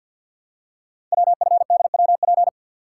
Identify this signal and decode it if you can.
Morse code, heard as a keyed tone, about 48 wpm, 690 Hz. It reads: WFDWP
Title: Code Groups - 5 Letters - 50wpm